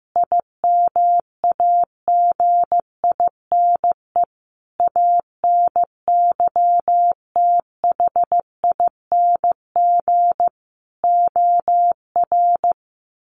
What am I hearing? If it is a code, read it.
Morse code, heard as a keyed tone, about 15 wpm, 700 Hz. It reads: IMAGINE ANYTHING OR
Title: Sentences from Top 1400 Words 15wpm